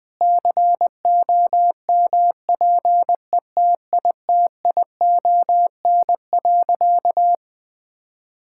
Morse code, heard as a keyed tone, about 20 wpm, 700 Hz. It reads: COMPETITION.
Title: Sentences with 6-Word Phrases - Spoken Beginning - 20wpm